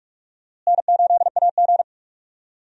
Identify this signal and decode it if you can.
Morse code, heard as a keyed tone, about 41 wpm, 700 Hz. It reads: N8AG
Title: USA Call Signs - 4 Characters - Repeated 2x - 45wpm